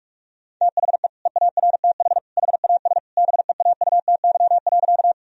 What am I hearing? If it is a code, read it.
Morse code, heard as a keyed tone, about 46 wpm, 700 Hz. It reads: THE EARTH HAS BEAUTY.
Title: Sentences from Top 1000 Words - Review Part 2 - 45wpm